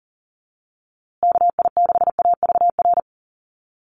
Morse code, heard as a keyed tone, about 40 wpm, 700 Hz. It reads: KI6AVR
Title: USA Call Signs Repeated 2x 40wpm